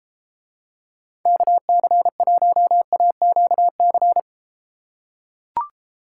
Morse code, heard as a keyed tone, about 30 wpm, 700 Hz. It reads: KC1AQC